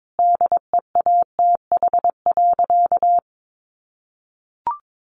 Morse code, heard as a keyed tone, about 22 wpm, 700 Hz. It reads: DEATH.
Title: Sentences from Top 1400 Words 22wpm